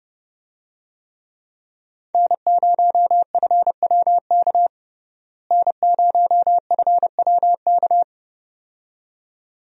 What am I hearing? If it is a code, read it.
Morse code, heard as a keyed tone, about 32 wpm, 700 Hz. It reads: N0FWK N0FWK
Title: USA Call Signs - 5 Characters - Repeated 2x - 30wpm